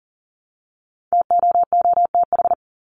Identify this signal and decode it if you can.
Morse code, heard as a keyed tone, about 40 wpm, 700 Hz. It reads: TOOTH
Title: Top 1751-2000 Words 40wpm